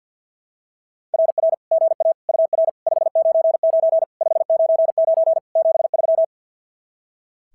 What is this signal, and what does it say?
Morse code, heard as a keyed tone, about 45 wpm, 660 Hz. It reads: RR GA UR 599 599 73
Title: SOTA QSO Practice 50wpm